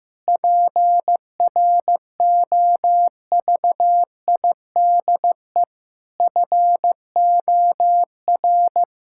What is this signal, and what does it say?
Morse code, heard as a keyed tone, about 15 wpm, 700 Hz. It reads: PROVIDE FOR
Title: Sentences from Top 1000 Words - Review Part 5 - 15wpm